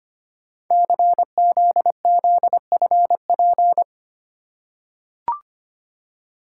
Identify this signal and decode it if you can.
Morse code, heard as a keyed tone, about 25 wpm, 700 Hz. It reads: CZZFP
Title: Code Groups - 5 Letter:Numbers - 25wpm